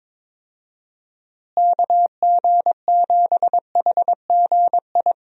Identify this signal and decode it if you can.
Morse code, heard as a keyed tone, about 21 wpm, 700 Hz. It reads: KG7HGI
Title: USA Call Signs - 6 Characters - Repeated 2x - 22wpm